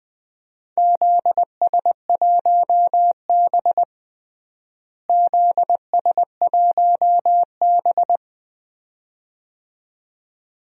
Morse code, heard as a keyed tone, about 20 wpm, 700 Hz. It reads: ZS1B ZS1B
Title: DXCC-100 Call Signs - 4 Characters - Repeated 2x - 20wpm